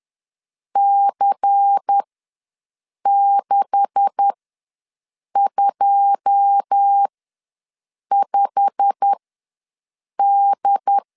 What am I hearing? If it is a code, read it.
Morse code, heard as a keyed tone, about 10 wpm, spaced as for 6 wpm, 800 Hz. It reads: C625D
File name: LEZIONI / 9 286 ws 10wpm 4032